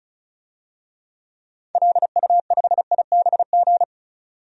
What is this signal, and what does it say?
Morse code, heard as a keyed tone, about 36 wpm, 700 Hz. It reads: LU5IBG
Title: DXCC-100 Call Signs - 6 Characters - 35wpm